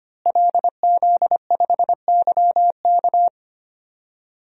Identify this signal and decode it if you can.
Morse code, heard as a keyed tone, about 25 wpm, 700 Hz. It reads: LZ5YK
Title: DXCC-100 Call Signs - 5 Characters - Repeated 2x - 25wpm